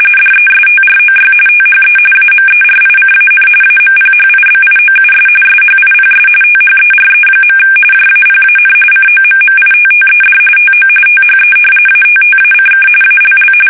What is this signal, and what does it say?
RTTY, about 45.45 baud. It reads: DESTROY THE AMERICAN AMPHIBIOUS SHIPS THERE. JAPAN COMMITTED TO THE ATTACK ALL
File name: fsk850 / battleshipnj